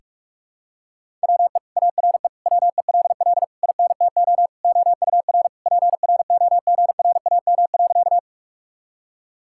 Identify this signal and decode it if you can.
Morse code, heard as a keyed tone, about 46 wpm, 700 Hz. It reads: WE ARE WELL INTO OUR PROGRAM.
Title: Sentences from Top 1000 Words - Review Part 5 - 45wpm